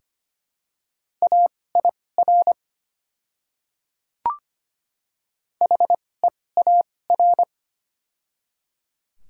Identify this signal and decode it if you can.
Morse code, heard as a keyed tone, about 25 wpm, spaced as for 18 wpm, 700 Hz. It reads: AIR HEAR
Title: 2x Character Spacing - Top 500 Words - 25wpm